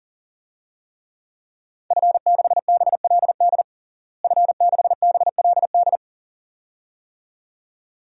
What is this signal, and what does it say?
Morse code, heard as a keyed tone, about 40 wpm, 690 Hz. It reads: F6BLD F6BLD
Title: DXCC-100 Call Signs - 5 Characters - Repeated 2x - 40wpm